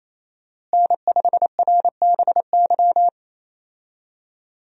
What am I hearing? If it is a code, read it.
Morse code, heard as a keyed tone, about 28 wpm, 700 Hz. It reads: N5RBY
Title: USA Call Signs - 5 Characters - Repeated 3x - 28wpm